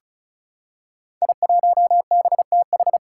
Morse code, heard as a keyed tone, about 36 wpm, 700 Hz. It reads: I1BTH
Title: Five Challenging Letter-Number Combinations 35wpm